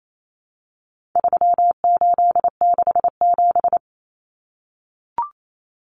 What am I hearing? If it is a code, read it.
Morse code, heard as a keyed tone, about 28 wpm, 700 Hz. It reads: 3867
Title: Numbers - 4-Digit 28wpm